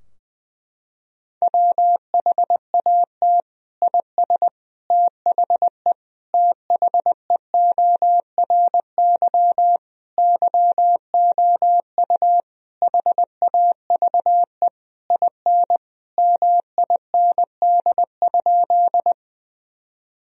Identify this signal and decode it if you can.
Morse code, heard as a keyed tone, about 21 wpm, 700 Hz. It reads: WHAT IS THE THEORY YOU HAVE IN MIND?
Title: Sentences from Top 1000 Words - Review Part 3 - 20wpm